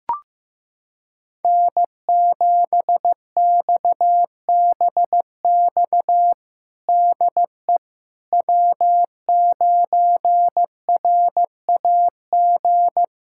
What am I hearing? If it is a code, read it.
Morse code, heard as a keyed tone, about 15 wpm, 700 Hz. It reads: N7XBX DE W9RAG
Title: USA Callsign DE Callsign - 5 Characters - 15wpm